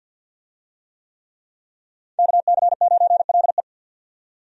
Morse code, heard as a keyed tone, about 51 wpm, 700 Hz. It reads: KC9LE
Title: USA Call Signs - 5 Characters - 50wpm